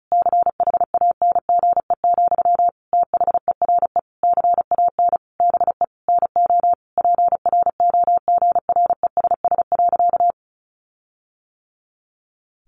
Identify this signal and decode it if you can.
Morse code, heard as a keyed tone, about 34 wpm, 700 Hz. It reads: CHANGE, THERE CAN BE NO PROGRESS.
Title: Sentences from Top 1400 Words 35wpm